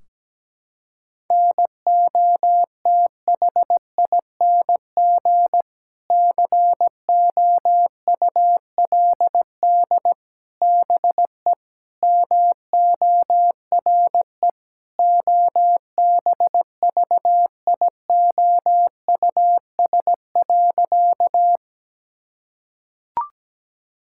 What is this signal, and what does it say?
Morse code, heard as a keyed tone, about 17 wpm, 700 Hz. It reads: NOTHING COULD BE MORE OBVIOUS.
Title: Sentences from Top 1600 Words 17wpm